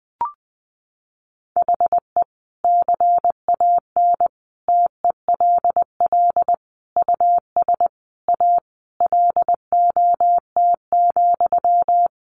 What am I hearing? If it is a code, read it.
Morse code, heard as a keyed tone, about 20 wpm, 700 Hz. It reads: HE CAN TELL US A LOT,
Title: Sentences from Top 1300 Words 20wpm